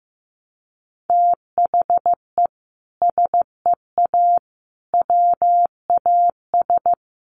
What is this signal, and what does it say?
Morse code, heard as a keyed tone, about 14 wpm, 700 Hz. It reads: THE SEA WAS
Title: Sentences from Top 1200 Words 15wpm